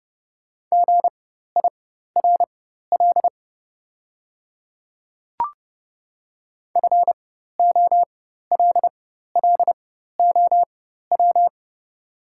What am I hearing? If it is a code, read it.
Morse code, heard as a keyed tone, about 31 wpm, spaced as for 14 wpm, 700 Hz. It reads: GIRL FOLLOW
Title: 4x Character Spacing - Top 500 Words - 30wpm